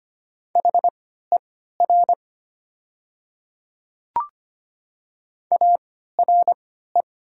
Morse code, heard as a keyed tone, about 25 wpm, spaced as for 14 wpm, 700 Hz. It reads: HER ARE
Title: 3x Character Spacing - Top 100 Words - 25wpm